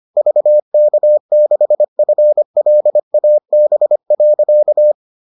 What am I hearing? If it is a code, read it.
Morse code, heard as a keyed tone, about 25 wpm, 600 Hz. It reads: VK6FLAB.